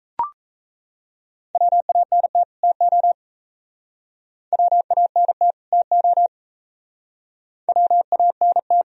Morse code, heard as a keyed tone, about 34 wpm, 690 Hz. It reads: WANT TO WANT TO WANT
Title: Speed-Racing - Sets of 2 Words 28wpm